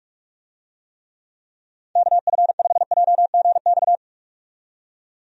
Morse code, heard as a keyed tone, about 41 wpm, 700 Hz. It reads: KF5JGX
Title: USA Call Signs - 6 Characters - 45wpm